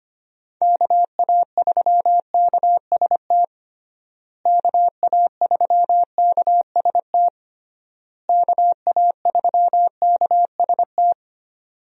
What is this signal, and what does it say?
Morse code, heard as a keyed tone, about 25 wpm, 700 Hz. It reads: KA3KST KA3KST KA3KST
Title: USA Call Signs - 6 Characters - Repeated 3x - 25wpm